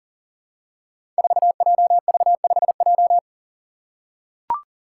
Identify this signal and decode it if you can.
Morse code, heard as a keyed tone, about 40 wpm, 700 Hz. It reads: 4JV5J
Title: Five Challenging Letter-Number Combinations 40wpm